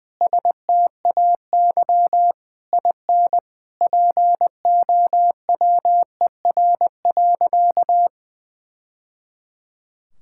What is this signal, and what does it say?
Morse code, heard as a keyed tone, about 20 wpm, 700 Hz. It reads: STAY IN POWER.
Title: Sentences from Top 1000 Words - Review Part 5 - 20wpm